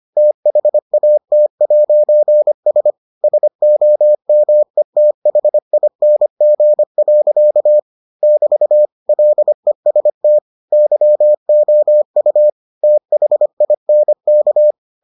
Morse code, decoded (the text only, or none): THAT'S SOMETHING. <BT> LEST YOU THINK